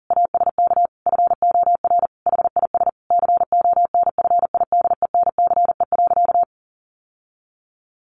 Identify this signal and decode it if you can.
Morse code, heard as a keyed tone, about 40 wpm, 700 Hz. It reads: ASK FOR HIS CONFIDENCE.
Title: Sentences from Top 1500 Words 40wpm